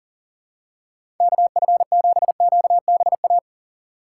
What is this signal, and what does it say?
Morse code, heard as a keyed tone, about 40 wpm, 700 Hz. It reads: KF7QBA
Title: USA Call Signs - 6 Characters - 4x Character Spacing - 40wpm